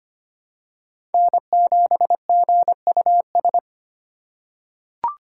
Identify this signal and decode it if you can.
Morse code, heard as a keyed tone, about 25 wpm, 700 Hz. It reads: N7GUS
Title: USA Call Signs - 5 Characters - 4x Character Spacing - 25wpm